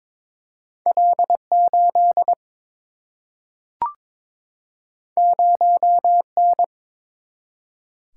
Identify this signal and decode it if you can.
Morse code, heard as a keyed tone, about 22 wpm, 700 Hz. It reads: L8 0N
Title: Single-Letter Single-Number 22wpm